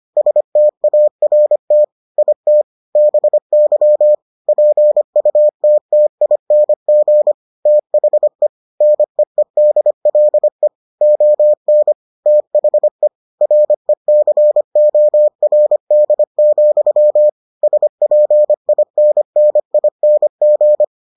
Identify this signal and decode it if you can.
Morse code, heard as a keyed tone, about 25 wpm, 600 Hz. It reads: START IT BY PUTTING THE NEEDLE ON THE RECORD, SPINNING